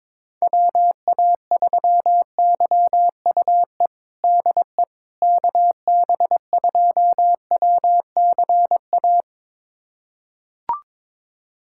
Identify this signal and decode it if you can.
Morse code, heard as a keyed tone, about 21 wpm, 700 Hz. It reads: WA3YUE DE KB2WCA
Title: USA Callsign DE Callsign 22wpm